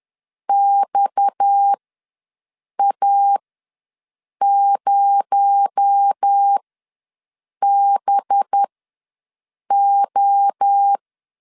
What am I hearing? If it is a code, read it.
Morse code, heard as a keyed tone, about 10 wpm, spaced as for 6 wpm, 800 Hz. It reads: XA0BO
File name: LEZIONI / 7 GF9Z / 7 G ws 10wpm 3843